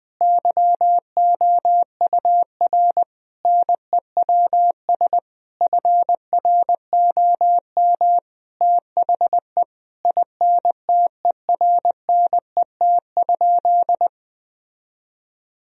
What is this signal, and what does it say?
Morse code, heard as a keyed tone, about 21 wpm, 700 Hz. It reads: YOUR NEWS FROM THE INTERNET?
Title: Sentences from Top 1000 Words - Review Part 4 - 20wpm